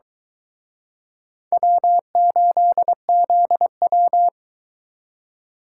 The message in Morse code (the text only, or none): W8ZW